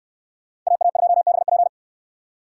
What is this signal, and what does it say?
Morse code, heard as a keyed tone, about 68 wpm, 700 Hz. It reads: AA2BP